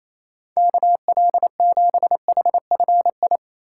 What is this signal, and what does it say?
Morse code, heard as a keyed tone, about 27 wpm, 700 Hz. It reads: KL7HFI
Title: K1USN SST QSO Practice - 28wpm